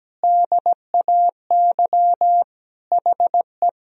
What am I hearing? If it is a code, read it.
Morse code, heard as a keyed tone, about 17 wpm, 700 Hz. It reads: DAY HE
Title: Sentences from Top 1000 Words - Review Part 4 - 17wpm